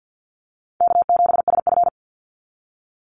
Morse code, heard as a keyed tone, about 50 wpm, 700 Hz. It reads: K7SF
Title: Easy Call Signs Repeated 3x - 50wpm